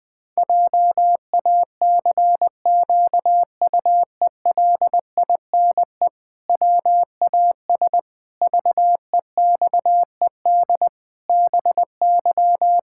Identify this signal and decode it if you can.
Morse code, heard as a keyed tone, about 20 wpm, 700 Hz. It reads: JACQUELINE WAS VEXED BY
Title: Pangram Challenge v2 - 20wpm